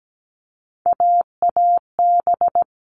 Morse code, heard as a keyed tone, about 17 wpm, 700 Hz. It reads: AAB
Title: All Cut Numbers - 3-Digit 17wpm